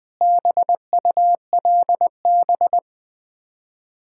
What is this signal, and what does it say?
Morse code, heard as a keyed tone, about 21 wpm, 700 Hz. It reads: BULB
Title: Difficult Words with Hints - 20wpm